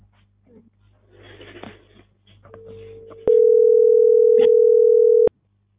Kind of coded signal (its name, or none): telephone call progress tone